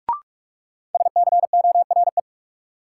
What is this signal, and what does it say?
Morse code, heard as a keyed tone, about 46 wpm, 700 Hz. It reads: SCORE
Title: Sentences from Top 1600 Words - New Word Drill Alt - 45wpm